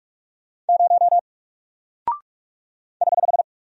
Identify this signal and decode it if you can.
Morse code, heard as a keyed tone, about 44 wpm, 700 Hz. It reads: 0 <HH>
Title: Single Character (Letter:Number:Punctuation:Prosign) Alt 45wpm